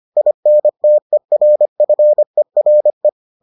Morse code, heard as a keyed tone, about 25 wpm, 600 Hz. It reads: INTERFERE